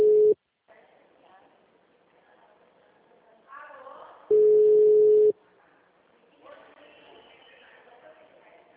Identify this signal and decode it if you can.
Ringback tone (Europe, 425 Hz), with speech.
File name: out- -2007- - - .46358